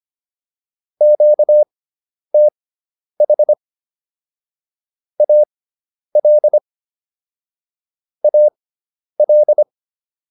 Morse code, heard as a keyed tone, about 25 wpm, spaced as for 10 wpm, 600 Hz. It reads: QTH AL AL
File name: QSO201 10